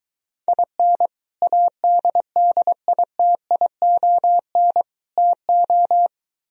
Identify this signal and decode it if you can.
Morse code, heard as a keyed tone, about 24 wpm, 700 Hz. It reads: IN ADDITION TO